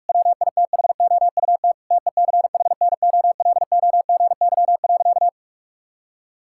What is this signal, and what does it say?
Morse code, heard as a keyed tone, about 41 wpm, 700 Hz. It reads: WITHOUT TECHNOLOGY.
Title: Sentences from Top 1000 Words - Review Part 5 - 45wpm